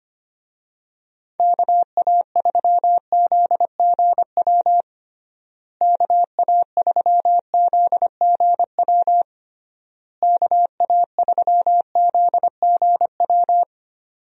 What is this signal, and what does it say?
Morse code, heard as a keyed tone, about 25 wpm, 700 Hz. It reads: KA3ZGW KA3ZGW KA3ZGW
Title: USA Call Signs - 6 Characters - Repeated 3x - 25wpm